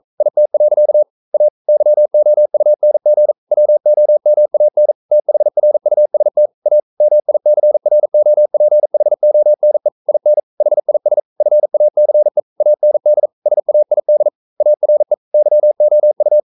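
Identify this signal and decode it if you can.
Morse code, heard as a keyed tone, about 42 wpm, 600 Hz. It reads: IT. A YOUNG WOMAN THRUST A MICROPHONE IN HIS FACE AND SAID ARE YOU
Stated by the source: Morse code practice at 41 wpm, from CW med Gnister.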